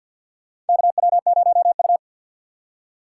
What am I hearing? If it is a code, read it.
Morse code, heard as a keyed tone, about 51 wpm, 700 Hz. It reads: KW0U
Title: USA Call Signs - 3x Character Spacing - 50wpm